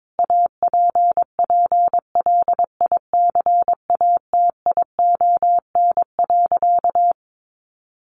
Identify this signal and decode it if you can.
Morse code, heard as a keyed tone, about 22 wpm, 700 Hz. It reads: APPLICATION.
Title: Sentences from Top 1200 Words 22wpm